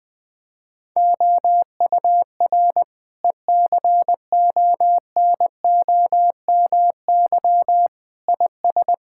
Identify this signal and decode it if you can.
Morse code, heard as a keyed tone, about 21 wpm, 700 Hz. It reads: OUR ECONOMY IS
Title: Sentences from Top 1000 Words - Review Part 6 - 20wpm